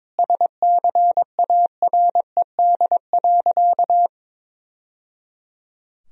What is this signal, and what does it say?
Morse code, heard as a keyed tone, about 21 wpm, 700 Hz. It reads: SCARED.
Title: Sentences from Top 1000 Words - Review Part 4 - 22wpm